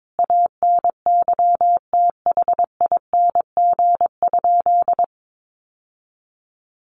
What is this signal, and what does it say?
Morse code, heard as a keyed tone, about 22 wpm, 700 Hz. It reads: ANYTHING?
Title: Sentences from Top 1000 Words 22wpm